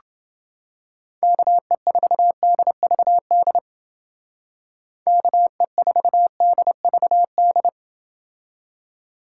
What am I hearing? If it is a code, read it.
Morse code, heard as a keyed tone, about 28 wpm, 700 Hz. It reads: KE4DVD KE4DVD